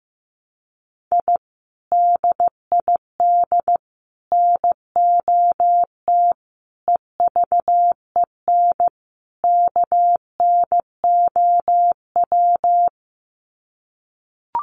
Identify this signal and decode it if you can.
Morse code, heard as a keyed tone, about 15 wpm, 700 Hz. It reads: I DID NOT EVEN KNOW
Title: Sets of 5 Words Encore 15wpm